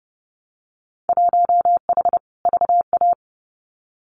Morse code, heard as a keyed tone, about 30 wpm, 700 Hz. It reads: 1H VA
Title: Winter Field Day Exchanges 30wpm